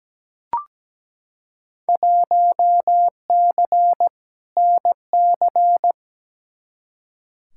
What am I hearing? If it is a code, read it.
Morse code, heard as a keyed tone, about 17 wpm, 700 Hz. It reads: 1C NC
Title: Field Day Exchanges - 17wpm